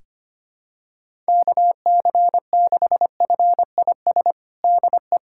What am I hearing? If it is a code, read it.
Morse code, heard as a keyed tone, about 25 wpm, 700 Hz. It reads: KC6FIS DE
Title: USA Callsign DE Callsign - 6 Characters - 25wpm